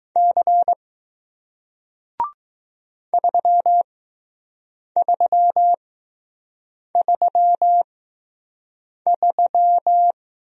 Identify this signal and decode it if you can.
Morse code, heard as a keyed tone, about 20 wpm, spaced as for 6 wpm, 700 Hz. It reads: C 3333